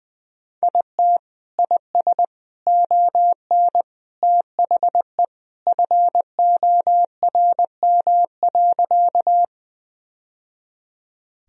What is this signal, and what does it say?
Morse code, heard as a keyed tone, about 20 wpm, 700 Hz. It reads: IT IS ON THE FORM.
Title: Sentences from Top 1000 Words - Review Part 1 - 20wpm